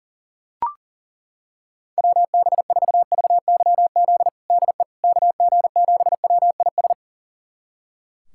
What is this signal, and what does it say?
Morse code, heard as a keyed tone, about 40 wpm, 700 Hz. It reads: WB4VYZ DE KG7WIS
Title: USA Callsign DE Callsign 40wpm